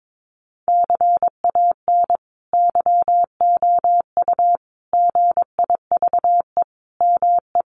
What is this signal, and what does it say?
Morse code, heard as a keyed tone, about 22 wpm, 700 Hz. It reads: CAN YOU GIVE ME
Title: Sentences from Top 1500 Words 22wpm